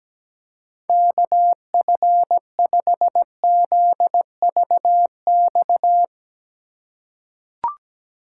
Morse code, heard as a keyed tone, about 17 wpm, 700 Hz. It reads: KF5ZVX
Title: USA Call Signs - 3x Character Spacing - 17wpm